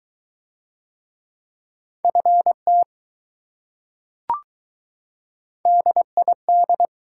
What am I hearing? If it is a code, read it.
Morse code, heard as a keyed tone, about 21 wpm, 700 Hz. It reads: FT DID